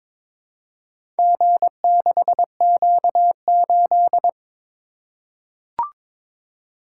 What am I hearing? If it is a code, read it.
Morse code, heard as a keyed tone, about 22 wpm, 700 Hz. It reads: G6Q8
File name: practice / 22wpm / Four Letter-Number Combinations - 22wpm